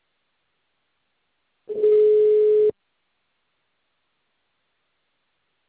Ringback tone (Europe, 425 Hz).
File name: out- -2030- - - .67886